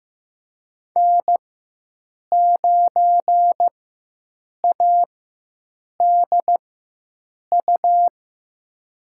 Morse code, heard as a keyed tone, about 16 wpm, spaced as for 7 wpm, 700 Hz. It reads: N9ADU
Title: USA Call Signs - 5 Characters - 4x Character Spacing - 15wpm